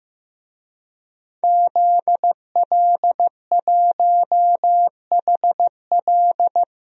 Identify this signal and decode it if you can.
Morse code, heard as a keyed tone, about 15 wpm, 700 Hz. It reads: ZL1HL
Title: DXCC-100 Call Signs - 5 Characters - 15wpm